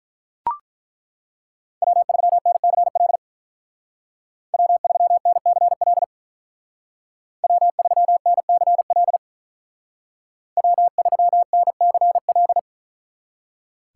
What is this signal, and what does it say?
Morse code, heard as a keyed tone, about 42 wpm, 700 Hz. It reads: W3NCL W3NCL W3NCL W3NCL
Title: Speed Racing - USA 1x3 Call Signs 35wpm